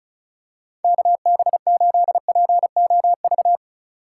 Morse code, heard as a keyed tone, about 36 wpm, 700 Hz. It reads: KB8POV
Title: USA Call Signs - 6 Characters - 4x Character Spacing - 35wpm